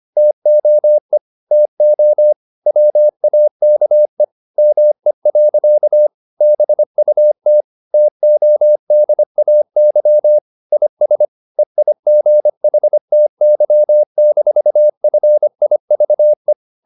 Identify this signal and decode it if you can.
Morse code, heard as a keyed tone, about 25 wpm, 600 Hz. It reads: TOE TO WAKE ME. BUT TODAY IS EIGHTY-FIVE
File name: the old man and the sea part 1 25wpm